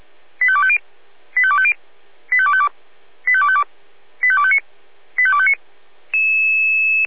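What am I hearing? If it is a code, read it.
Five-tone selective call (ZVEI), heard as quick runs of five tones, tones of about 70 ms. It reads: 85250 85250 85252 85252 85259 85259
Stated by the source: ZVEI five-tone selective call received via SDR.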